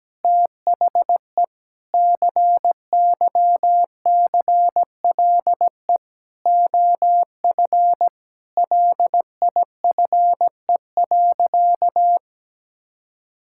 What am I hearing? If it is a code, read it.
Morse code, heard as a keyed tone, about 17 wpm, 700 Hz. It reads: THE CYCLE OF LIFE.
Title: Sentences from Top 1000 Words - Review Part 6 - 17wpm